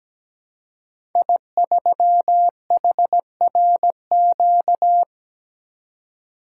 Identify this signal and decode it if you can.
Morse code, heard as a keyed tone, about 17 wpm, 700 Hz. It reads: I3HRQ
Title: Code Groups - 5 Letter:Numbers - 17wpm